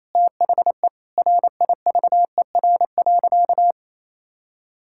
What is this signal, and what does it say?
Morse code, heard as a keyed tone, about 27 wpm, 700 Hz. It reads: THE RIVER.
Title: Sentences from Top 1600 Words 28wpm